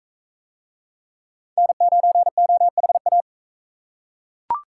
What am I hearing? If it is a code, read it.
Morse code, heard as a keyed tone, about 42 wpm, 700 Hz. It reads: N9OHA